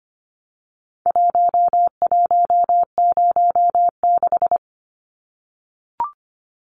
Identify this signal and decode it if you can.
Morse code, heard as a keyed tone, about 25 wpm, 700 Hz. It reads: 1106
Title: Numbers - Contest Style - 4-Digit 25wpm